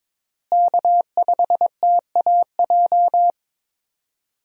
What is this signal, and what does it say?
Morse code, heard as a keyed tone, about 22 wpm, 700 Hz. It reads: K5TAJ
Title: USA Call Signs - 5 Characters - Repeated 3x - 22wpm